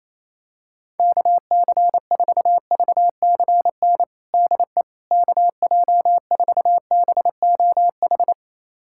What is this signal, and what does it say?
Morse code, heard as a keyed tone, about 28 wpm, 700 Hz. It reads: KC4VCN DE KJ4BOH
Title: USA Callsign DE Callsign - 6 Characters - 28wpm